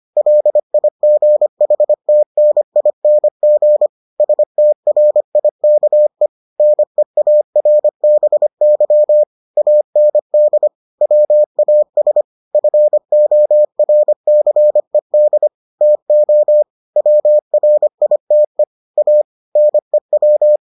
Morse code, heard as a keyed tone, about 25 wpm, 600 Hz. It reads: LIGHTNING STRIKE NEARBY AND WAS FORCED TO WRITE A NEW